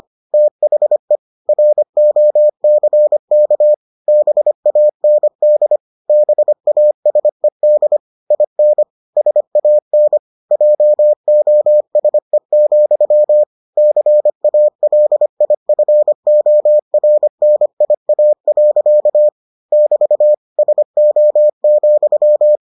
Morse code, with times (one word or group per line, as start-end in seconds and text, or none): 0.3-1.2 THE
1.5-3.7 ROCK
4.1-5.8 BAND
6.1-8.0 BASED
8.3-8.8 IN
9.2-10.2 SAN
10.5-13.4 JOSE,
13.8-19.3 CALIFORNIA.
19.7-20.3 <BT>
20.6-22.6 SO,